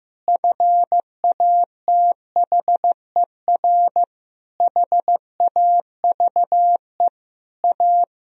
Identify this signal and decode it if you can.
Morse code, heard as a keyed tone, about 15 wpm, 700 Hz. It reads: FATHER HAVE A
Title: Sentences from Top 1000 Words - Review Part 3 - 15wpm